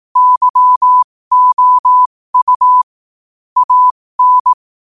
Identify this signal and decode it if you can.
Morse code, heard as a keyed tone, about 18 wpm, spaced as for 15 wpm, 1000 Hz. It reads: YOU AN
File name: qotdfiles / 15wpm